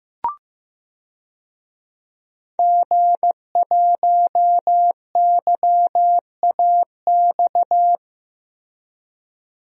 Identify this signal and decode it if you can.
Morse code, heard as a keyed tone, about 15 wpm, 700 Hz. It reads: G1YAX
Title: DXCC-100 Call Signs - 5 Characters - Repeated 3x - 15wpm